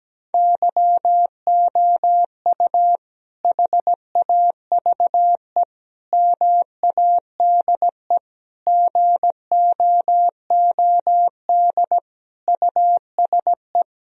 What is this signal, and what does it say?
Morse code, heard as a keyed tone, about 17 wpm, 700 Hz. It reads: YOU HAVE MADE GOOD USE
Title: Sentences from Top 1000 Words - Review Part 1 - 17wpm